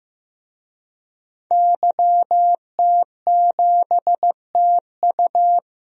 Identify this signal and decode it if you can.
Morse code, heard as a keyed tone, about 15 wpm, 700 Hz. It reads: YT7TU
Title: DXCC-100 Call Signs - 5 Characters - 15wpm